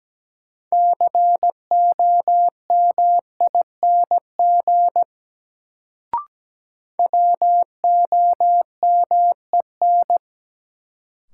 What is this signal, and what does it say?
Morse code, heard as a keyed tone, about 17 wpm, 700 Hz. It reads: COMING WOMEN
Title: Sentences from Top 1300 Words - New Word Drill - 17wpm